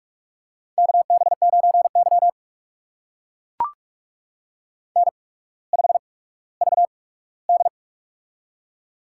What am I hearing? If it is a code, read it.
Morse code, heard as a keyed tone, about 44 wpm, spaced as for 19 wpm, 700 Hz. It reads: KB9Y N5VD
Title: USA Call Signs - 4 Characters - 8x Character Spacing - 45wpm